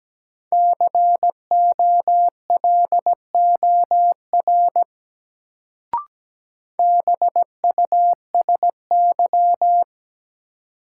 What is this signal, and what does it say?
Morse code, heard as a keyed tone, about 17 wpm, 700 Hz. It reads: COLOR BUSY
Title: Sentences from Top 1000 Words - Review Part 2 - New Word Drill - 17wpm